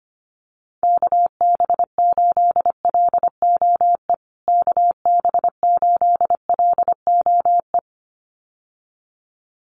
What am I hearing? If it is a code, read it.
Morse code, heard as a keyed tone, about 25 wpm, 700 Hz. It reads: KB8LOE KB8LOE
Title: USA Call Signs Repeated 2x 25wpm